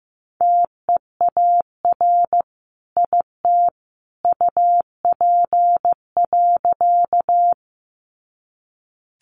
Morse code, heard as a keyed tone, about 15 wpm, 700 Hz. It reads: TEAR IT UP.
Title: A Sound Of Thunder 15wpm